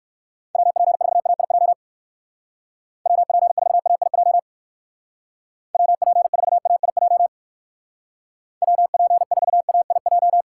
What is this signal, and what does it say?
Morse code, heard as a keyed tone, about 59 wpm, 700 Hz. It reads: WP4AIJ WP4AIJ WP4AIJ WP4AIJ